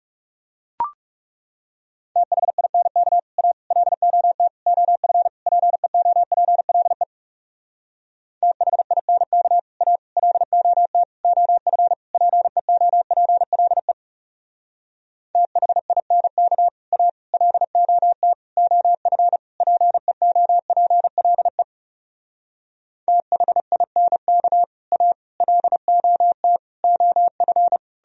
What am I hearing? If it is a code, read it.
Morse code, heard as a keyed tone, about 39 wpm, 700 Hz. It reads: THINK A LOT OF PEOPLE THINK A LOT OF PEOPLE THINK A LOT OF PEOPLE THINK A LOT OF